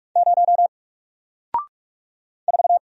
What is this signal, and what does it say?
Morse code, heard as a keyed tone, about 44 wpm, 700 Hz. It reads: 0 4
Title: Single Character (Letter:Number:Punctuation:Prosign) 45wpm